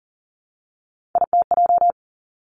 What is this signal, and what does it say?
Morse code, heard as a keyed tone, about 42 wpm, 700 Hz. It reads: ITJ
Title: Three Challenging Letter-Number Combinations 40wpm